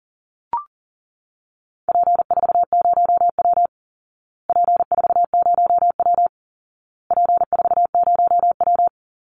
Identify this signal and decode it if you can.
Morse code, heard as a keyed tone, about 40 wpm, 700 Hz. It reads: P40W P40W P40W
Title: Call Signs Repeated 2x - 40wpm